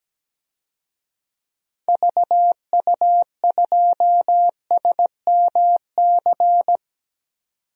Morse code, heard as a keyed tone, about 17 wpm, 700 Hz. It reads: VU2SMC
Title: DXCC-100 Call Signs - 6 Characters - 17wpm